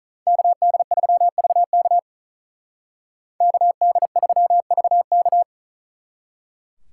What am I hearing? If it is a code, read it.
Morse code, heard as a keyed tone, about 46 wpm, 700 Hz. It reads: KD3VK KD3VK